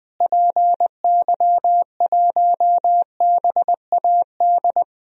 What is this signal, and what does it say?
Morse code, heard as a keyed tone, about 20 wpm, 700 Hz. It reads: PY1BAD
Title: DXCC-100 Call Signs - 6 Characters - Repeated 3x - 20wpm